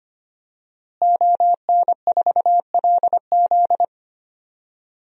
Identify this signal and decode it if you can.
Morse code, heard as a keyed tone, about 25 wpm, 700 Hz. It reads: ON4LZ
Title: DXCC-100 Call Signs - 5 Characters - Repeated 3x - 25wpm